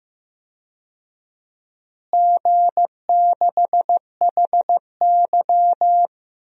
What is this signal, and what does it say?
Morse code, heard as a keyed tone, about 15 wpm, 700 Hz. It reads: G6HY
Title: DXCC-100 Call Signs - 4 Characters - Repeated 2x - 15wpm